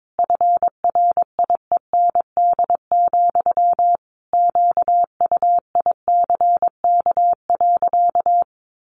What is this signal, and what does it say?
Morse code, heard as a keyed tone, about 22 wpm, 700 Hz. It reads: FRIEND, QUICK.
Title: Sentences from Top 500 Words 22wpm